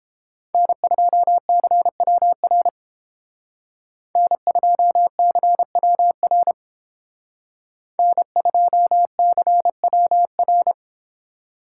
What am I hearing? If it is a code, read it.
Morse code, heard as a keyed tone, about 28 wpm, 700 Hz. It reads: N2CWR N2CWR N2CWR